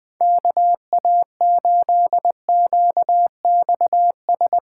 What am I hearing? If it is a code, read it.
Morse code, heard as a keyed tone, about 22 wpm, 700 Hz. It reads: KA8QXS
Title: Speed Racing - USA 2x3 Call Signs 20wpm